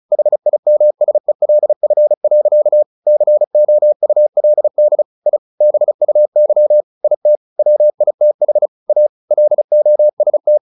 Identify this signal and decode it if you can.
Morse code, heard as a keyed tone, about 34 wpm, 600 Hz. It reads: HIMSELF. COULD I BUY IT WITH A LOST
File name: the old man and the sea 2 35wpm